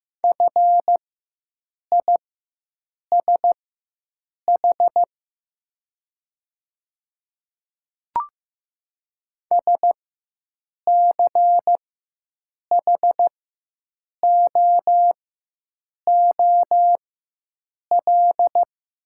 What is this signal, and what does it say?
Morse code, heard as a keyed tone, about 15 wpm, spaced as for 7 wpm, 700 Hz. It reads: FISH SCHOOL
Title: 4x Character Spacing - Top 500 Words - 15wpm